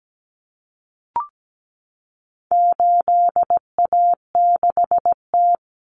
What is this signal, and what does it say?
Morse code, heard as a keyed tone, about 17 wpm, 700 Hz. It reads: 8A6T
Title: Common Cut Numbers - 4-Digit 17wpm